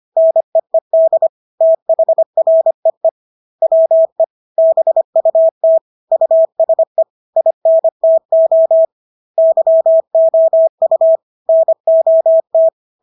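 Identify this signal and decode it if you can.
Morse code, heard as a keyed tone, about 25 wpm, 650 Hz. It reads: NEED THREE WE BUT USE INTO YOU NOT
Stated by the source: Morse practice of English words at 25 wpm from CW med Gnister.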